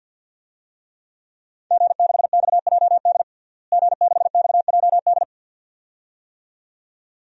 Morse code, heard as a keyed tone, about 46 wpm, 700 Hz. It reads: G6XJD G6XJD
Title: DXCC-100 Call Signs - 5 Characters - Repeated 2x - 50wpm